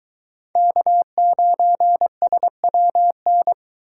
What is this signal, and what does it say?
Morse code, heard as a keyed tone, about 23 wpm, 700 Hz. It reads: K9SWN